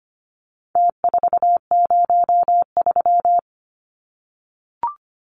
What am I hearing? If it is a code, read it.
Morse code, heard as a keyed tone, about 25 wpm, 700 Hz. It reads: T403
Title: Numbers - Contest Style - 4-Digit 25wpm